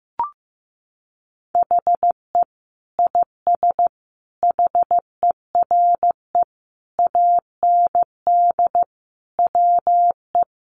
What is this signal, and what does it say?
Morse code, heard as a keyed tone, about 15 wpm, 700 Hz. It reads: HE IS HERE AND WE
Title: Sentences from Top 800 Words 15wpm